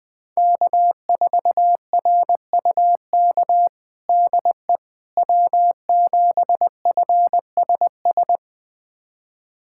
Morse code, heard as a keyed tone, about 20 wpm, 700 Hz. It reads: K4RUK DE W7FSS
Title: USA Callsign DE Callsign - 5 Characters - 20wpm